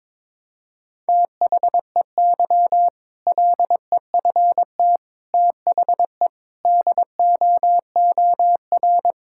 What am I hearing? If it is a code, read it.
Morse code, heard as a keyed tone, about 22 wpm, 700 Hz. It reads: THEY LEFT THE DOOR
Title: Sentences from Top 1000 Words - Review Part 6 - 22wpm